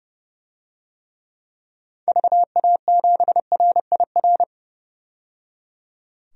Morse code, heard as a keyed tone, about 30 wpm, 700 Hz. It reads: VA7RIR
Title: DXCC-100 Call Signs - 6 Characters - 30wpm